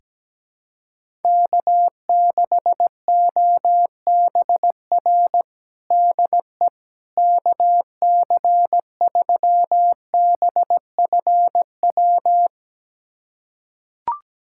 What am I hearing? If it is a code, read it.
Morse code, heard as a keyed tone, about 17 wpm, 700 Hz. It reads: K6OBR DE KC3BFW
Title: USA Callsign DE Callsign 17wpm